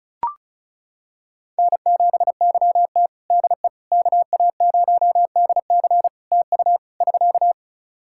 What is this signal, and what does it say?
Morse code, heard as a keyed tone, about 34 wpm, 700 Hz. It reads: N7YT DE KA0DC TU <SK>
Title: Practice QSO - Exchange 4 - Ending - 35wpm